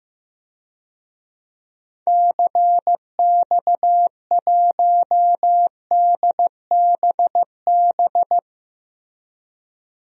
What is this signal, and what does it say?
Morse code, heard as a keyed tone, about 15 wpm, 700 Hz. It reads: CX1DBB
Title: DXCC-100 Call Signs - 6 Characters - Repeated 2x - 15wpm